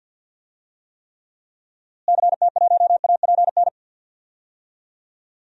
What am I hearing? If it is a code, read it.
Morse code, heard as a keyed tone, about 50 wpm, 700 Hz. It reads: CT1APN
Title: DXCC-100 Call Signs - 6 Characters - Repeated 2x - 50wpm